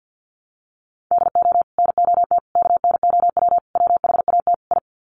Morse code, heard as a keyed tone, about 50 wpm, 700 Hz. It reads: DO NOT KNOW WHAT I
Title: Sets of 5 Words 50wpm